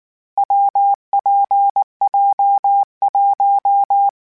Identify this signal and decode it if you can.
Morse code, heard as a keyed tone, about 19 wpm, 800 Hz. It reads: WPJ1
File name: Difficult Combos / Difficult Combos-mp3s / WPJ WPJ1 @18